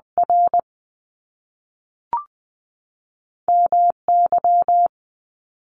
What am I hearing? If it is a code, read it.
Morse code, heard as a keyed tone, about 20 wpm, spaced as for 3 wpm, 700 Hz. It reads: RM Y
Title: Top Two Letter Combinations in Words 20wpm